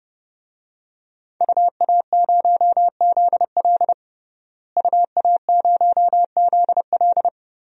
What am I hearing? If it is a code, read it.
Morse code, heard as a keyed tone, about 32 wpm, 700 Hz. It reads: UA0ZL UA0ZL
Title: DXCC-100 Call Signs - 5 Characters - Repeated 3x - 30wpm